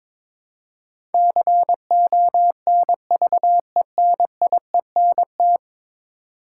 Morse code, heard as a keyed tone, about 22 wpm, 700 Hz. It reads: CONVENIENT
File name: practice / 22wpm / 10 Letter Words - First 3 Letters Given - 22wpm